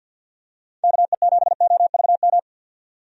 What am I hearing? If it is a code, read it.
Morse code, heard as a keyed tone, about 52 wpm, 700 Hz. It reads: KE7OVM
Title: USA Call Signs - 6 Characters - 50wpm